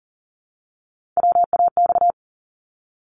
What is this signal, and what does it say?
Morse code, heard as a keyed tone, about 40 wpm, 700 Hz. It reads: WAX
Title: Multi-letter Phonemes Repeated 3x 40wpm